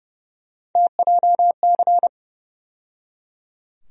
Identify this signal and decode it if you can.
Morse code, heard as a keyed tone, about 30 wpm, 700 Hz. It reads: TJC
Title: USA Call Sign Postfixes - 3 Letters 30wpm